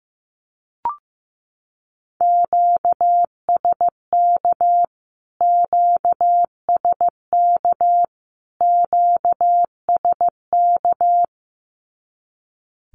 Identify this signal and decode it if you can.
Morse code, heard as a keyed tone, about 15 wpm, 700 Hz. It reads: QSK QSK QSK
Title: Q-Codes Repeated 3x 15wpm